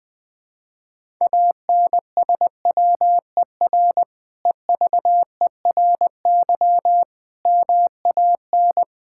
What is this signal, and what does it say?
Morse code, heard as a keyed tone, about 20 wpm, 700 Hz. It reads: ANSWER EVERY MAN
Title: Sentences from Top 1000 Words - Review Part 5 - 20wpm